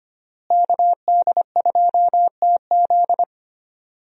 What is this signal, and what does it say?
Morse code, heard as a keyed tone, about 25 wpm, 700 Hz. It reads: KD2TZ
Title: USA Call Signs - 5 Characters - Repeated 3x - 25wpm